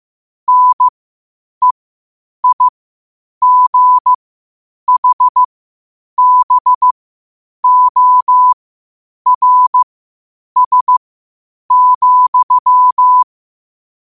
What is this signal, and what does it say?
Morse code, heard as a keyed tone, about 15 wpm, spaced as for 8 wpm, 1000 Hz. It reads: NEIGHBORS,
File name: qotdfiles / 8wpm